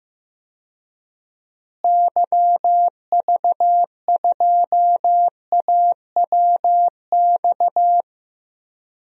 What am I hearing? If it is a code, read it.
Morse code, heard as a keyed tone, about 15 wpm, 700 Hz. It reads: YV2AWX
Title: DXCC-100 Call Signs - 6 Characters - Repeated 2x - 15wpm